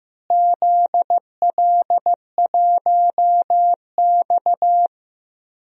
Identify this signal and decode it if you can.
Morse code, heard as a keyed tone, about 15 wpm, 700 Hz. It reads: ZL1X
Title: DXCC-100 Call Signs - 4 Characters - Repeated 2x - 15wpm